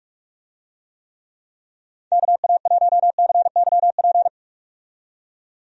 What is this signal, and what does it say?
Morse code, heard as a keyed tone, about 46 wpm, 700 Hz. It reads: KA1CYP
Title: USA Call Signs - 6 Characters - 45wpm